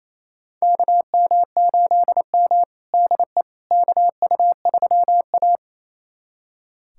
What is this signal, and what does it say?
Morse code, heard as a keyed tone, about 28 wpm, 700 Hz. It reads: KM8M DE KU3A
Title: USA Callsign DE Callsign - 4 Characters - 28wpm